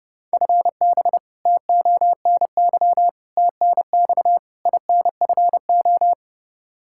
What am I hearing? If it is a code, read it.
Morse code, heard as a keyed tone, about 30 wpm, 700 Hz. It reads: FB TONY TNX INFO
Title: Practice QSO - Exchange 3 - More Info - 30wpm